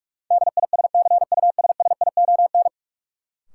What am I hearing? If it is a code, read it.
Morse code, heard as a keyed tone, about 47 wpm, 700 Hz. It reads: DISCUSSION
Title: Sentences from Top 1100 Words - New Word Drill Alt - 45wpm